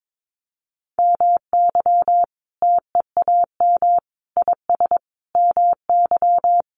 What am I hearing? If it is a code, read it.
Morse code, heard as a keyed tone, about 22 wpm, 700 Hz. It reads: MY TEAM IS MY
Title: Sentences from Top 1500 Words 22wpm